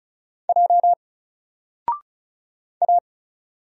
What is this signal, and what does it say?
Morse code, heard as a keyed tone, about 36 wpm, 700 Hz. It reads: J A
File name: Single-Letter-Number-35wpm